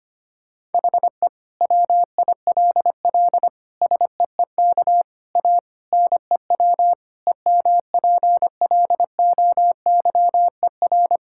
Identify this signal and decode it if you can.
Morse code, heard as a keyed tone, about 25 wpm, 700 Hz. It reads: HE WILL SEEK A NEW EMPLOYER
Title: Sentences from Top 1600 Words 25wpm